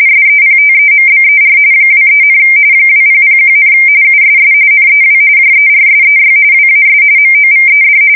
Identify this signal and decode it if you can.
RTTY, about 45.45 baud. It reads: SEPTEMBER 23, 2001, USS NEW JERSEY WAS MOVED
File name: fsk170 / battleshipnj